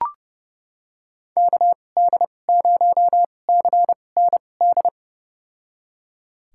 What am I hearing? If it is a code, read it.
Morse code, heard as a keyed tone, about 30 wpm, spaced as for 22 wpm, 700 Hz. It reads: KD0CND
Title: USA Call Signs - 6 Characters - 2x Character Spacing - 30wpm